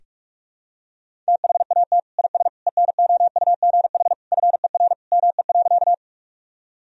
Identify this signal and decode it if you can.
Morse code, heard as a keyed tone, about 46 wpm, 700 Hz. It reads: THAT IS ENOUGH FER ME.
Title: Sentences from Top 400 Words and Abbreviations 45wpm